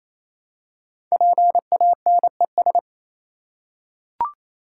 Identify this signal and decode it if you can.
Morse code, heard as a keyed tone, about 29 wpm, 700 Hz. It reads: PANES
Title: Code Groups - 5 Letters - 28wpm